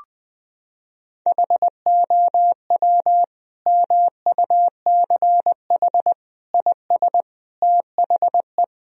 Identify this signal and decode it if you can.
Morse code, heard as a keyed tone, about 20 wpm, 700 Hz. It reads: HOW MUCH IS THE
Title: Sentences from Top 1000 Words - Review Part 1 - 20wpm